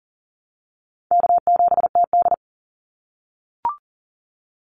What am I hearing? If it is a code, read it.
Morse code, heard as a keyed tone, about 40 wpm, 700 Hz. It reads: K7TD
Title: Call Signs - 40wpm